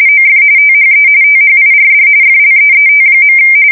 RTTY, about 45.45 baud. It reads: OF THE THIRD FLEET.
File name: fsk170 / battleshipnj